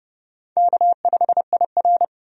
Morse code, heard as a keyed tone, about 30 wpm, 700 Hz. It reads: K5IR
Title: USA Call Signs - 4 Characters - 30wpm